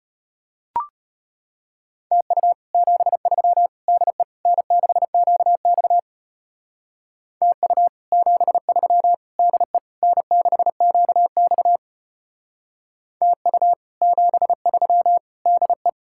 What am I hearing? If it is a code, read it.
Morse code, heard as a keyed tone, about 30 wpm, 700 Hz. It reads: TU 73 DE N6QX TU 73 DE N6QX TU 73 DE
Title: Speed-Racing Common QSO Phrases Encore - 25wpm